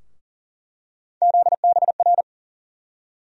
Morse code, heard as a keyed tone, about 40 wpm, 700 Hz. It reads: ZBR
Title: USA Call Sign Postfixes - 3 Letters 40wpm